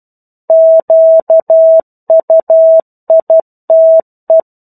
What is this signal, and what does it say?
Morse code, heard as a keyed tone, about 12 wpm, 650 Hz. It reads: QUITE
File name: 01-Chapter-01 12 WPM